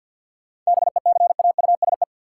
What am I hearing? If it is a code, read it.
Morse code, heard as a keyed tone, about 52 wpm, 700 Hz. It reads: BECAUSE
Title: Sets of 2 Words - New Word Drill - 50wpm